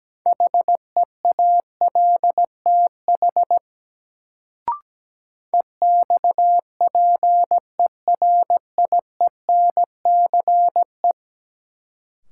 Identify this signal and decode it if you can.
Morse code, heard as a keyed tone, about 17 wpm, 700 Hz. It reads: HEALTH EXPERIENCE
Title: Sentences from Top 600 Words - New Word Drill - 17wpm